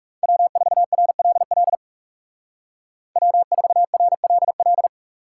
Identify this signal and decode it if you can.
Morse code, heard as a keyed tone, about 38 wpm, 700 Hz. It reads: W4RLL W4RLL